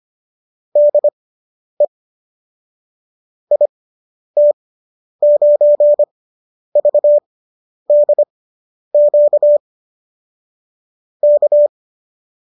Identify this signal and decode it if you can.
Morse code, heard as a keyed tone, about 25 wpm, spaced as for 10 wpm, 590 Hz. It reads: DE IT9VDQ K